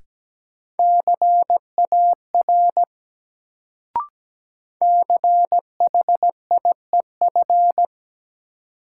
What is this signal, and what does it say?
Morse code, heard as a keyed tone, about 17 wpm, 700 Hz. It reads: CAR CHIEF
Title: Sets of 5 Words Encore - New Word Drill - 17wpm